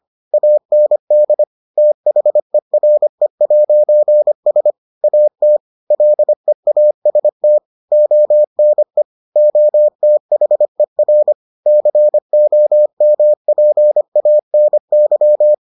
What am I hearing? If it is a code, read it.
Morse code, heard as a keyed tone, about 25 wpm, 600 Hz. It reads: AND THERE'S AT LEAST ONE OTHER COMPANY